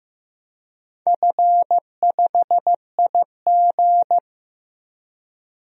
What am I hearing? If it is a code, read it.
Morse code, heard as a keyed tone, about 15 wpm, 700 Hz. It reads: F5IG
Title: DXCC-100 Call Signs - 4 Characters - 15wpm